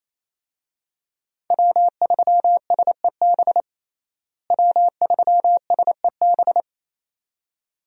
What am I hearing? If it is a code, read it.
Morse code, heard as a keyed tone, about 27 wpm, 700 Hz. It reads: W3SEB W3SEB
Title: USA Call Signs - 5 Characters - Repeated 2x - 28wpm